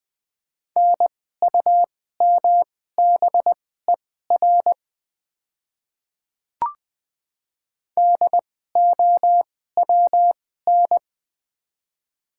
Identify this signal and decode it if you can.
Morse code, heard as a keyed tone, about 21 wpm, spaced as for 15 wpm, 700 Hz. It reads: NUMBER DOWN
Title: 2x Character Spacing - Top 100 Words - 20wpm